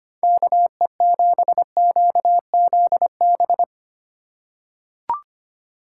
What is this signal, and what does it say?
Morse code, heard as a keyed tone, about 25 wpm, 700 Hz. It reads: KE7QZB
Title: USA Call Signs - 6 Characters - 4x Character Spacing - 25wpm